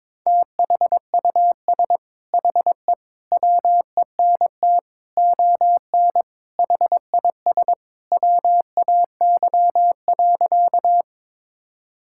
Morse code, heard as a keyed tone, about 22 wpm, 700 Hz. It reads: THUS HE WENT ON HIS WAY.
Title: Sentences from Top 1000 Words - Review Part 5 - 22wpm